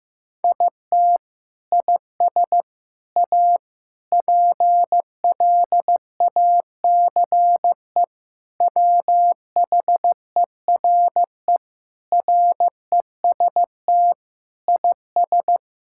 Morse code, heard as a keyed tone, about 15 wpm, 700 Hz. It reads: IT IS A PLACE WHERE REST IS
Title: Sentences from Top 1000 Words - Review Part 6 - 15wpm